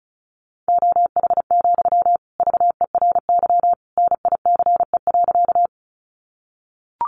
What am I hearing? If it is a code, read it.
Morse code, heard as a keyed tone, about 34 wpm, 700 Hz. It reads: OH, VERY NICE.
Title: Sentences from Top 1100 Words 35wpm